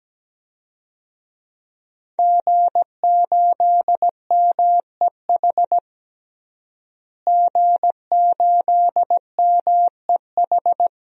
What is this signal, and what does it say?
Morse code, heard as a keyed tone, about 17 wpm, 700 Hz. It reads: G8MEH G8MEH
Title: DXCC-100 Call Signs - 5 Characters - Repeated 3x - 17wpm